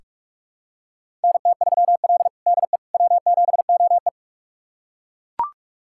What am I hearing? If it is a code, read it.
Morse code, heard as a keyed tone, about 41 wpm, 700 Hz. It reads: NT3L DE W7OE
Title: USA Callsign DE Callsign - 4 Characters - 45wpm